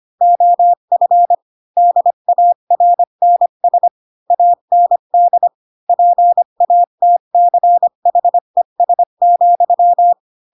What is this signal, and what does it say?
Morse code, heard as a keyed tone, about 25 wpm, 700 Hz. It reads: OF DARNS AND PATCHES,
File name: Pinocchio / Pinocchio ch8 20 25 ik1hge